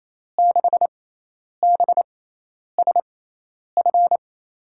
Morse code, heard as a keyed tone, about 28 wpm, spaced as for 10 wpm, 700 Hz. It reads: 6BSF
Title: USA Call Signs - 6 Characters - 6x Character Spacing - 28wpm